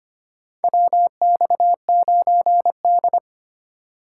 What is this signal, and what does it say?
Morse code, heard as a keyed tone, about 25 wpm, 700 Hz. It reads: WX9D
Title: USA Call Signs - 4 Characters - 4x Character Spacing - 25wpm